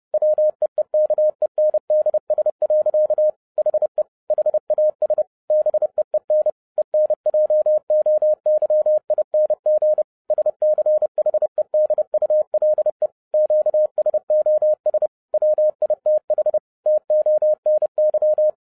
Morse code, heard as a keyed tone, about 30 wpm, 610 Hz. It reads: WEEKENDS. HE HAS BEEN ENJOYING SCHEDULE QSOS WITH TONY
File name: 30wpm